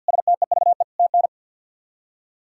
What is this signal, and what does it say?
Morse code, heard as a keyed tone, about 48 wpm, 700 Hz. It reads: STEVE TN
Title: K1USN SST QSO Practice - 50wpm